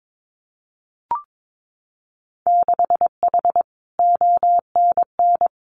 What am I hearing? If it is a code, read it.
Morse code, heard as a keyed tone, about 22 wpm, 700 Hz. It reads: 6H ONN
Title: Winter Field Day Exchanges Encore 22wpm